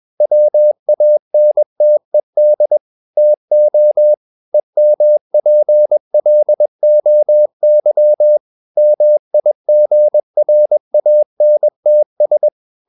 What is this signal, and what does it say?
Morse code, heard as a keyed tone, about 21 wpm, 600 Hz. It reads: WANTED TO EMPLOY MIGRANTS
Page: CW med Gnister